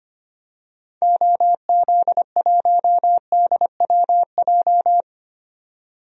Morse code, heard as a keyed tone, about 25 wpm, 700 Hz. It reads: OZ1DWJ
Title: DXCC-100 Call Signs - 6 Characters - 25wpm